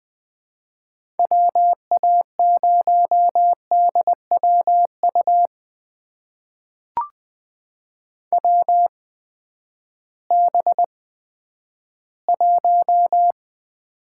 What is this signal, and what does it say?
Morse code, heard as a keyed tone, about 21 wpm, spaced as for 11 wpm, 700 Hz. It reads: WA0DWU WB1
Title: USA Call Signs - 6 Characters - 8x Character Spacing - 20wpm